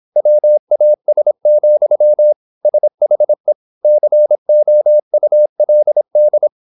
Morse code, heard as a keyed tone, about 26 wpm, 600 Hz. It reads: WAS, SHE COULD
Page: Never 26wpm | CW med Gnister